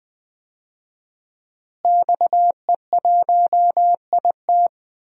Morse code, heard as a keyed tone, about 20 wpm, 700 Hz. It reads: XE1IT
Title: DXCC-100 Call Signs - 5 Characters - Repeated 2x - 20wpm